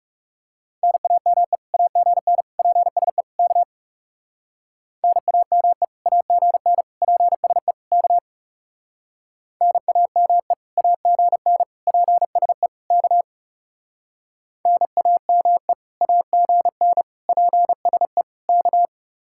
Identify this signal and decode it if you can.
Morse code, heard as a keyed tone, about 38 wpm, 700 Hz. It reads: NAME AGN PSE K NAME AGN PSE K NAME AGN PSE K NAME AGN PSE K